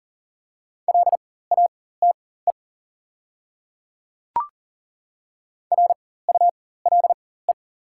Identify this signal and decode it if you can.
Morse code, heard as a keyed tone, about 40 wpm, spaced as for 19 wpm, 690 Hz. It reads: LATE RULE
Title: 4x Character Spacing - Top 500 Words - 40wpm